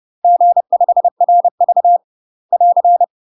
Morse code, heard as a keyed tone, about 30 wpm, 700 Hz. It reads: G5RV <AR>
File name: UK MASTER QSOs TRAINING / 30WPM / UK MASTER QSOs 0030